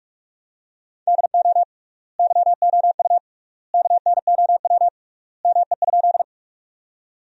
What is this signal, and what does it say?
Morse code, heard as a keyed tone, about 46 wpm, 700 Hz. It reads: DO YOU KNOW ME?
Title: 3x Word-Spacing - Sentences from Top 200 Words - 45wpm